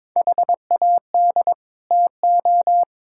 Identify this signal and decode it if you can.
Morse code, heard as a keyed tone, about 22 wpm, 700 Hz. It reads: HAD TO
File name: Build A Phrase - 2-Word Phrases - 22wpm